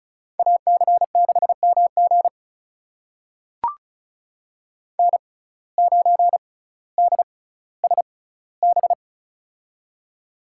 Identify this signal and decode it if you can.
Morse code, heard as a keyed tone, about 34 wpm, spaced as for 18 wpm, 700 Hz. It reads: AC6MG N9DSB
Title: USA Call Signs - 5 Characters - 6x Character Spacing - 35wpm